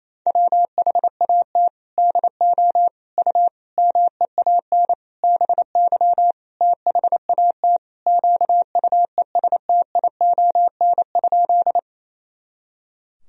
Morse code, heard as a keyed tone, about 28 wpm, 700 Hz. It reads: WHAT DO U MEAN BY THAT QUESTION?
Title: Sentences from Top 400 Words and Abbreviations 28wpm